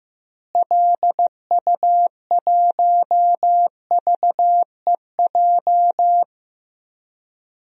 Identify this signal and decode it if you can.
Morse code, heard as a keyed tone, about 15 wpm, 700 Hz. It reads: LU1VEJ
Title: DXCC-100 Call Signs - 6 Characters - Repeated 3x - 15wpm